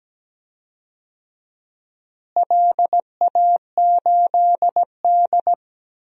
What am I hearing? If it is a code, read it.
Morse code, heard as a keyed tone, about 17 wpm, 700 Hz. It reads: LA8D
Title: DXCC-100 Call Signs - 4 Characters - Repeated 2x - 17wpm